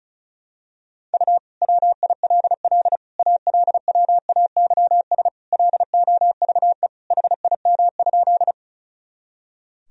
Morse code, heard as a keyed tone, about 36 wpm, 700 Hz. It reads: U WILL ALWAYS LOVE HIM?
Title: Sentences from Top 400 Words and Abbreviations 35wpm